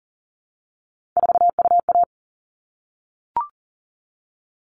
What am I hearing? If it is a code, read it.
Morse code, heard as a keyed tone, about 40 wpm, 700 Hz. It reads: 4UA
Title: All Cut Numbers - 3-Digit 40wpm